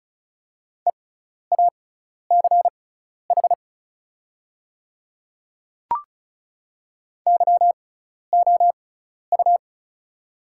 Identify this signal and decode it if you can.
Morse code, heard as a keyed tone, about 36 wpm, spaced as for 12 wpm, 700 Hz. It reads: EACH YOU
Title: 6x Character Spacing - Top 300 Words - 35wpm